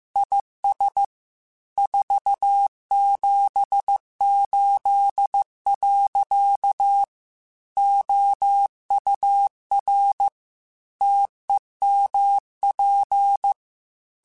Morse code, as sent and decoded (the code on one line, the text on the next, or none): .. ... / ....- --... ---.. .-.-.- / --- ..- .-. / - . -- .--.
IS 478. OUR TEMP